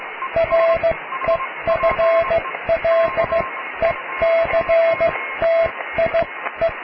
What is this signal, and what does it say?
Morse code, heard as a keyed tone, about 16 wpm, spaced as for 13 wpm, 660 Hz. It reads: REFLECTIE
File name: cw / tekst(a)+QRM